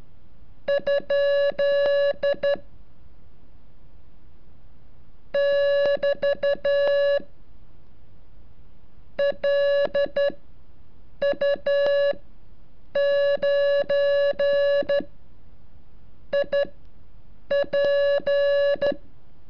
Morse code, heard as a keyed tone, about 12 wpm, spaced as for 6 wpm, 570 Hz. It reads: ? <BT> LU9IP
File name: curso / ejercicio 34